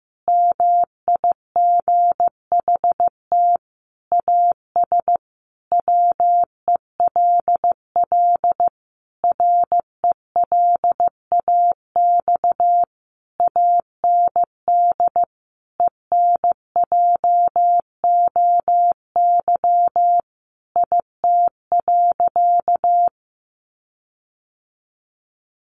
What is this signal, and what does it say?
Morse code, heard as a keyed tone, about 15 wpm, 700 Hz. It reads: MIGHT AS WELL RELAX AND ENJOY IT.
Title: Sentences from Top 1400 Words 15wpm